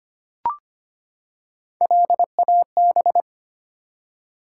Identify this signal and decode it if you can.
Morse code, heard as a keyed tone, about 25 wpm, 700 Hz. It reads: LAB
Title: Three Letter Words - 25wpm